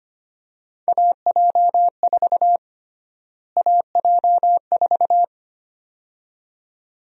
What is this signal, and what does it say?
Morse code, heard as a keyed tone, about 25 wpm, 700 Hz. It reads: AJ4 AJ4
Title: USA Call Sign Prefixes - Repeated 2x - 25wpm